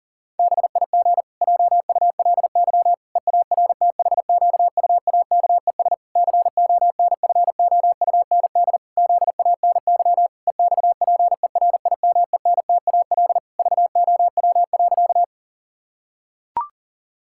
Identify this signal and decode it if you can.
Morse code, heard as a keyed tone, about 40 wpm, 700 Hz. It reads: BIG JULY EARTHQUAKES CONFOUND ZANY EXPERIMENTAL VOW.
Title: Pangram Challenge v2 - 40wpm